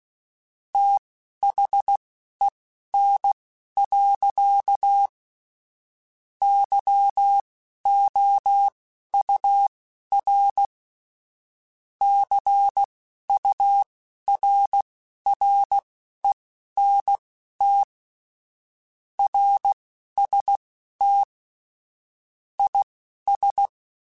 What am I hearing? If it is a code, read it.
Morse code, heard as a keyed tone, about 16 wpm, spaced as for 11 wpm, 800 Hz. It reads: THEN. YOUR CURRENT RST IS